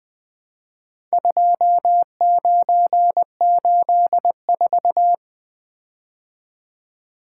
Morse code, heard as a keyed tone, about 23 wpm, 700 Hz. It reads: 2984
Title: Speed-Racing - Four Letter-Number Combinations - 20wpm